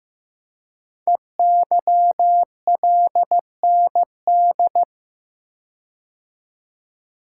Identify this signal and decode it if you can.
Morse code, heard as a keyed tone, about 15 wpm, 700 Hz. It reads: EYLND
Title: Code Groups - 5 Letters - 15wpm